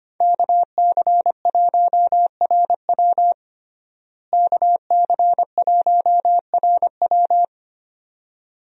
Morse code, heard as a keyed tone, about 25 wpm, 700 Hz. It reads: KC1RW KC1RW
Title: USA Call Signs - 5 Characters - Repeated 2x - 25wpm